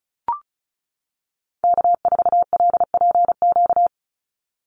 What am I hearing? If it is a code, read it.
Morse code, heard as a keyed tone, about 36 wpm, 700 Hz. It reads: K4LPQ
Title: NA SOTA Activator Callsigns 35wpm